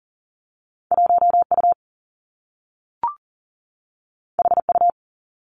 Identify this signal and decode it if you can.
Morse code, heard as a keyed tone, about 43 wpm, 700 Hz. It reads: 1 U H U
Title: Two Challenging Letter-Number Combinations 40wpm